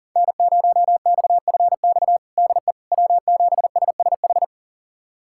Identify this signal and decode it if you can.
Morse code, heard as a keyed tone, about 40 wpm, 700 Hz. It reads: N0XFX DE W7SSH
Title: USA Callsign DE Callsign - 5 Characters - 40wpm